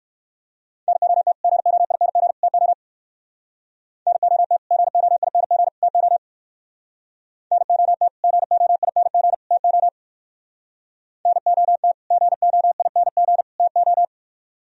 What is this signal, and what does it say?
Morse code, heard as a keyed tone, about 61 wpm, 700 Hz. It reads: NOT GOING TO NOT GOING TO NOT GOING TO NOT GOING TO